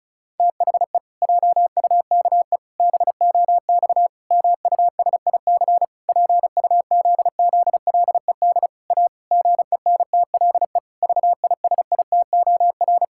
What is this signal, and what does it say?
Morse code, heard as a keyed tone, about 34 wpm, 700 Hz. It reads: THE JUKE BOX MUSIC PUZZLED A GENTLE VISITOR
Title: Pangram Challenge v2 - 35wpm